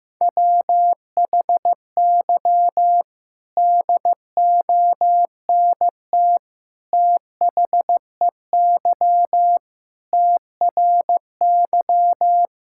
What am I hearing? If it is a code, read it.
Morse code, heard as a keyed tone, about 14 wpm, 700 Hz. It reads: WHY DONT THEY TRY
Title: Sentences from Top 1000 Words - Review Part 5 - 15wpm